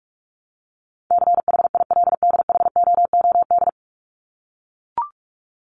Morse code, heard as a keyed tone, about 44 wpm, 700 Hz. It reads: CHILDHOOD
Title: Top 1751-2000 Words 45wpm